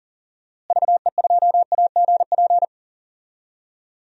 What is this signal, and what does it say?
Morse code, heard as a keyed tone, about 37 wpm, 700 Hz. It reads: VE2AGP
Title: DXCC-100 Call Signs - 6 Characters - Repeated 2x - 40wpm